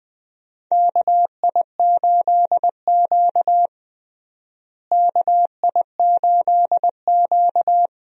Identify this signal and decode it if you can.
Morse code, heard as a keyed tone, about 21 wpm, 700 Hz. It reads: KI8Q KI8Q
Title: USA Call Signs - 4 Characters - Repeated 2x - 20wpm